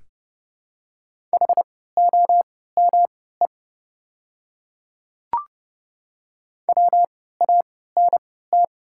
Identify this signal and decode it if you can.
Morse code, heard as a keyed tone, about 29 wpm, spaced as for 17 wpm, 700 Hz. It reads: HOME WANT
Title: 3x Character Spacing - Top 200 Words - 30wpm